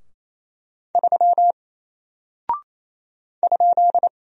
Morse code, heard as a keyed tone, about 28 wpm, 700 Hz. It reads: 3 ?
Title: Single Character (Letter:Number:Punctuation:Prosign) Alt 28wpm